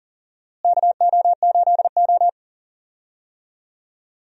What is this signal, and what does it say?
Morse code, heard as a keyed tone, about 40 wpm, 700 Hz. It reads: KO8O
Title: USA Call Signs - 4 Characters - Repeated 2x - 40wpm